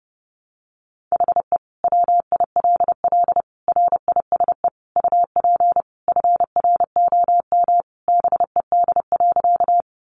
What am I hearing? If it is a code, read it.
Morse code, heard as a keyed tone, about 30 wpm, 700 Hz. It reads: HE WILL RISE UP FROM BED.
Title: Sentences from Top 900 Words 30wpm